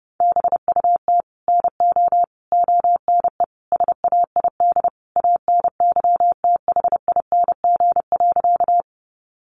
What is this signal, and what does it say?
Morse code, heard as a keyed tone, about 30 wpm, 700 Hz. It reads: BUT NO ONE SAID ANYTHING.
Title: Sentences from Top 1000 Words 30wpm